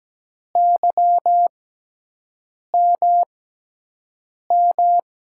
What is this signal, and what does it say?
Morse code, heard as a keyed tone, about 17 wpm, spaced as for 6 wpm, 700 Hz. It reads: YMM
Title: USA Call Signs - 6x Character Spacing - 17wpm